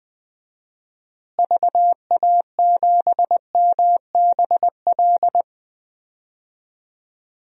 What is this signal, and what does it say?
Morse code, heard as a keyed tone, about 20 wpm, 700 Hz. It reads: VA7MBL
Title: DXCC-100 Call Signs - 6 Characters - Repeated 2x - 20wpm